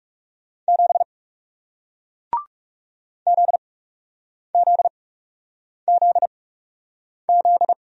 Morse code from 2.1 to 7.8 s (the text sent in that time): ZZZZ